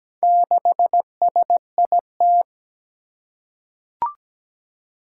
Morse code, heard as a keyed tone, about 17 wpm, 700 Hz. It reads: 6SIT
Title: Four Challenging Letter-Number Combinations 17wpm